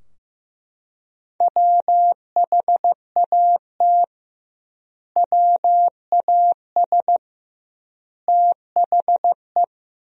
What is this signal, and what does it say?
Morse code, heard as a keyed tone, about 16 wpm, 700 Hz. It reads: WHAT WAS THE
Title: 2x Word-Spacing - Sentences from Top 300 Words - 15wpm